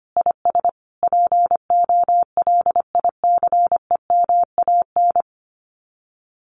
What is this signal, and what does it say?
Morse code, heard as a keyed tone, about 25 wpm, 700 Hz. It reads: IS POLICEMAN
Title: Common QSO Phrases Encore 25wpm